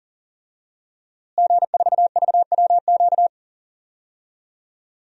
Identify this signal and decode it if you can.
Morse code, heard as a keyed tone, about 40 wpm, 700 Hz. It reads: G4VWQ
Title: DXCC-100 Call Signs - 5 Characters - Repeated 2x - 40wpm